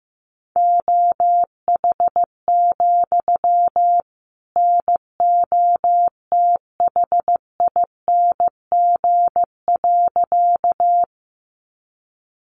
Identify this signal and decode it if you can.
Morse code, heard as a keyed tone, about 15 wpm, 700 Hz. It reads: OH, NOTHING.
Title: Sentences from Top 500 Words 15wpm